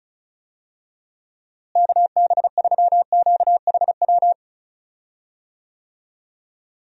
Morse code, heard as a keyed tone, about 34 wpm, 700 Hz. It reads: KB3QHW
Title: USA Call Signs - 6 Characters - Repeated 2x - 35wpm